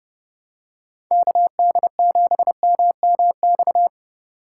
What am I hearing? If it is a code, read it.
Morse code, heard as a keyed tone, about 30 wpm, 700 Hz. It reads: KD7MMX
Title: USA Call Signs - 6 Characters - 3x Character Spacing - 30wpm